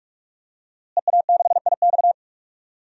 Morse code, heard as a keyed tone, about 46 wpm, 700 Hz. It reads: EA6IX
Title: DXCC-100 Call Signs - 5 Characters - Repeated 3x - 45wpm